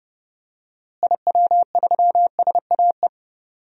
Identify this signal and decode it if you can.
Morse code, heard as a keyed tone, about 30 wpm, 700 Hz. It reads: IW3SAE
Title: DXCC-100 Call Signs - 6 Characters - Repeated 3x - 30wpm